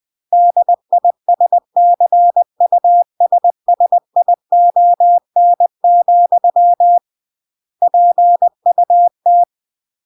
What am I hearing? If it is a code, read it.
Morse code, heard as a keyed tone, about 20 wpm, 700 Hz. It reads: DISCUSSION, PUT
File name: CW / jodorowsky / 82 commandments 20-20